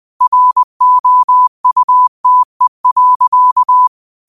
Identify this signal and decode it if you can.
Morse code, heard as a keyed tone, about 20 wpm, 1000 Hz. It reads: ROUTE.